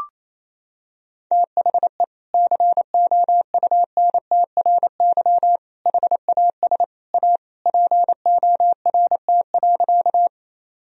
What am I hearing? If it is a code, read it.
Morse code, heard as a keyed tone, about 28 wpm, 700 Hz. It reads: THE COUNTRY HAS A PORT.
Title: Sentences from Top 1000 Words - Review Part 1 - 28wpm